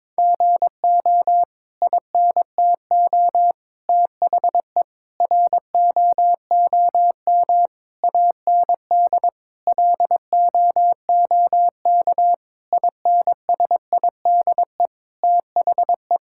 Morse code, heard as a keyed tone, about 22 wpm, 700 Hz. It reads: GO INTO THE ROOM AND LOOK INSIDE THE
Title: Sentences from Top 1000 Words - Review Part 6 - 22wpm